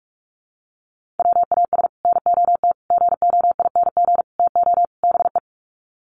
Morse code, heard as a keyed tone, about 44 wpm, 700 Hz. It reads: WAS NOT GOING TO BE
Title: Sets of 5 Words Encore 45wpm